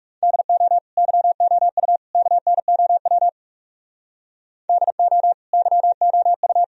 Morse code, heard as a keyed tone, about 36 wpm, 700 Hz. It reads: DO YOU KNOW DO YOU
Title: Speed-Racing - Sets of 3 Words 30wpm